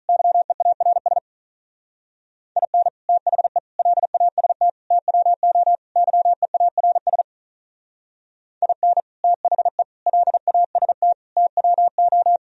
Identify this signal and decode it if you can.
Morse code, heard as a keyed tone, about 45 wpm, 700 Hz. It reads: YEARS IN THE LAST TWO YEARS IN THE LAST TWO